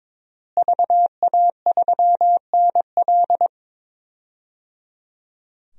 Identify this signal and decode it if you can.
Morse code, heard as a keyed tone, about 22 wpm, 700 Hz. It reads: VA3NL
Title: DXCC-100 Call Signs - 5 Characters - Repeated 3x - 22wpm